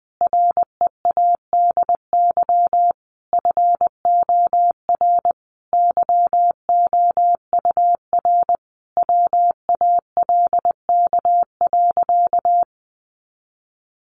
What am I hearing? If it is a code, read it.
Morse code, heard as a keyed tone, about 20 wpm, 700 Hz. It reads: READY FOR YOUR WALK.
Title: Sentences from Top 800 Words 20wpm